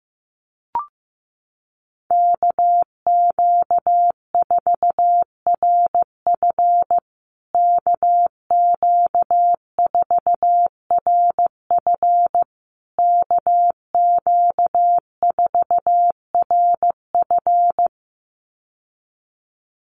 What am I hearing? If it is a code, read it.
Morse code, heard as a keyed tone, about 15 wpm, 700 Hz. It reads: KQ4RF KQ4RF KQ4RF
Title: USA Call Signs Repeated 3x 15wpm